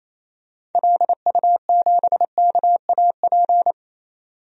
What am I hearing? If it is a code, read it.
Morse code, heard as a keyed tone, about 28 wpm, 700 Hz. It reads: LU7KAP
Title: DXCC-100 Call Signs - 6 Characters - 28wpm